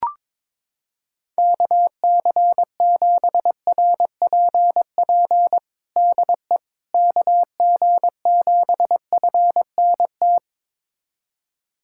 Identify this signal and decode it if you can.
Morse code, heard as a keyed tone, about 22 wpm, 700 Hz. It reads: KC7RPP DE KG7FNT
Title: USA Callsign DE Callsign - 6 Characters - 22wpm